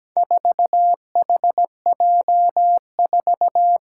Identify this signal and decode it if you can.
Morse code, heard as a keyed tone, about 17 wpm, 700 Hz. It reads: 4HJ4
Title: Four Challenging Letter-Number Combinations 17wpm